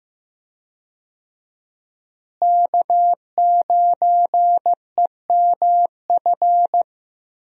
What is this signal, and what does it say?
Morse code, heard as a keyed tone, about 15 wpm, 700 Hz. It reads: K9EMF
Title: USA Call Signs - 5 Characters - Repeated 2x - 15wpm